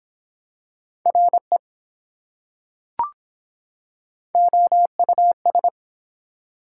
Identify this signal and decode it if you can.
Morse code, heard as a keyed tone, about 24 wpm, 700 Hz. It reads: RE OUS